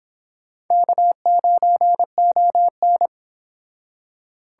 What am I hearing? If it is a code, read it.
Morse code, heard as a keyed tone, about 25 wpm, 700 Hz. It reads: K9ON